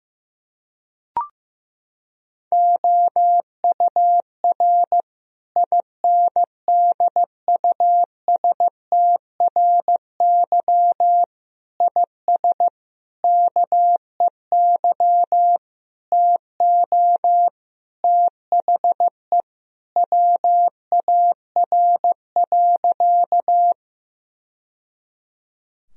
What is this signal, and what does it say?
Morse code, heard as a keyed tone, about 15 wpm, 700 Hz. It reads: OUR INDUSTRY IS KEY TO THE WAR.
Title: Sentences from Top 1000 Words - Review Part 4 - 15wpm